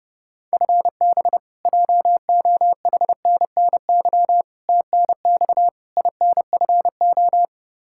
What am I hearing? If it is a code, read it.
Morse code, heard as a keyed tone, about 32 wpm, 700 Hz. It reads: FB JOHNNY TNX INFO
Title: Practice QSO - Exchange 3 - More Info - 30wpm